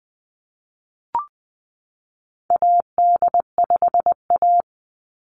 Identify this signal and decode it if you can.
Morse code, heard as a keyed tone, about 20 wpm, 700 Hz. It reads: AD5A
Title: NA SOTA Chaser Callsigns 20wpm